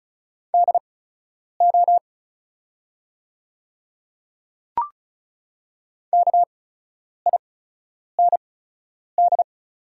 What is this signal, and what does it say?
Morse code, heard as a keyed tone, about 36 wpm, spaced as for 10 wpm, 690 Hz. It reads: DO KIND
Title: 8x Character Spacing - Top 200 Words - 35wpm